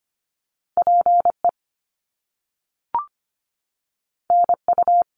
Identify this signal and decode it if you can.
Morse code, heard as a keyed tone, about 25 wpm, 700 Hz. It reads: PE NU
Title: Canadian Provinces and Territories 25wpm